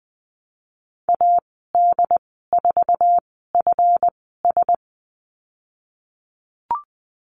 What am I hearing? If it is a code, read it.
Morse code, heard as a keyed tone, about 20 wpm, spaced as for 15 wpm, 700 Hz. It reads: AD4FS
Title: 161 Lesson 26g - Letter g - Call Signs 20wpm